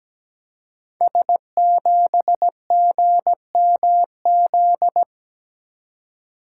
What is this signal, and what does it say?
Morse code, heard as a keyed tone, about 17 wpm, 700 Hz. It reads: S7GMZ
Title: Code Groups - 5 Letter:Numbers - 17wpm